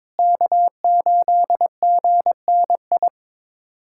Morse code, heard as a keyed tone, about 21 wpm, 700 Hz. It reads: K8GNI
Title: USA Callsign DE Callsign - 5 Characters - 22wpm